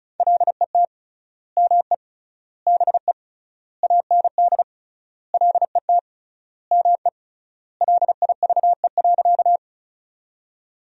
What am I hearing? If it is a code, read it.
Morse code, heard as a keyed tone, about 36 wpm, 700 Hz. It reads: LET ME BE AND LET ME LIVE.
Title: 3x Word-Spacing - Sentences from Top 300 Words - 35wpm